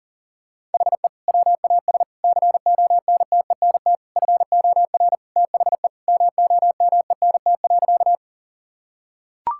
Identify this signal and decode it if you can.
Morse code, heard as a keyed tone, about 40 wpm, 700 Hz. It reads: HE WAS CONTENT FOR THE MOMENT.
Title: Sentences from Top 1000 Words - Review Part 6 - 40wpm